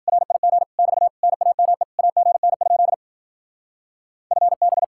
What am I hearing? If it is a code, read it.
Morse code, heard as a keyed tone, about 50 wpm, 700 Hz. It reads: RIG <BT> NAME AGN? FB